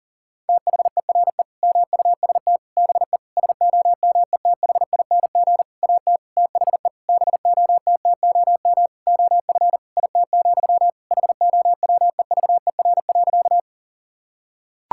Morse code, heard as a keyed tone, about 37 wpm, 700 Hz. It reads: THERE MUST BE SOMETHING AT THE BOTTOM OF IT, HOWEVER.
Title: Sentences from Top 1000 Words - Review Part 6 - 40wpm